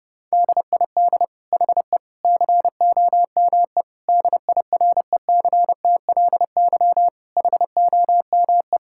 Morse code, heard as a keyed tone, about 31 wpm, 700 Hz. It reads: DID HE COME DIRECTLY HOME
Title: Sentences from Top 1000 Words - Review Part 5 - 30wpm